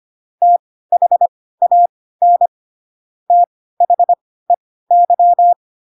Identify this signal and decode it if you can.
Morse code, heard as a keyed tone, about 25 wpm, spaced as for 16 wpm, 700 Hz. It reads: THAN THEY
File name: Pinocchio / Pinocchio ch4 10 25 ik1hge